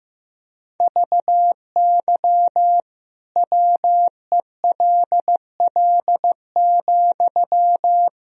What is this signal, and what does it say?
Morse code, heard as a keyed tone, about 15 wpm, 700 Hz. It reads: VY WELL,
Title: Sentences from Top 500 Words and Abbreviations 15wpm